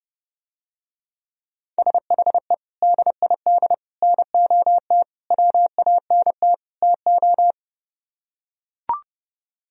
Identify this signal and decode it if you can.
Morse code, heard as a keyed tone, about 30 wpm, 700 Hz. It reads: SHE DID NOT WANT TO